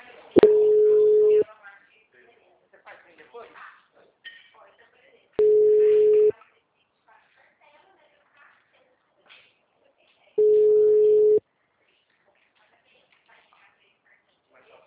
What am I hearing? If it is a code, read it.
Ringback tone (Europe, 425 Hz), with speech.